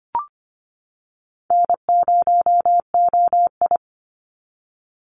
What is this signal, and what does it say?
Morse code, heard as a keyed tone, about 25 wpm, 700 Hz. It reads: N0OI
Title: NA SOTA Chaser Callsigns 25wpm